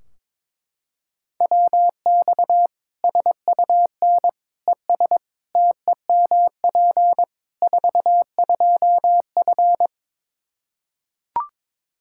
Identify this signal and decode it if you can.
Morse code, heard as a keyed tone, about 21 wpm, 700 Hz. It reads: WX SUN ES TEMP 42F
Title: K1USN SST QSO Practice 22wpm